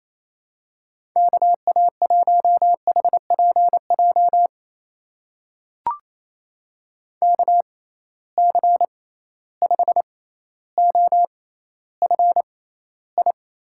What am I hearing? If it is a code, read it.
Morse code, heard as a keyed tone, about 28 wpm, spaced as for 14 wpm, 700 Hz. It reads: KA1HPJ KC5OFI
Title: USA Call Signs - 6 Characters - 6x Character Spacing - 28wpm